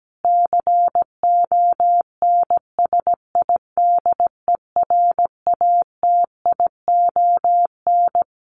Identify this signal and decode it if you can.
Morse code, heard as a keyed tone, about 17 wpm, 700 Hz. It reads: CONSIDERATION
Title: Sentences from Top 1500 Words 17wpm